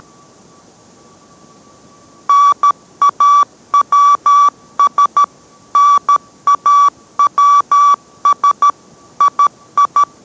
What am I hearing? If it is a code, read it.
Morse code, heard as a keyed tone, about 13 wpm, 1150 Hz. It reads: NAWS NAWS II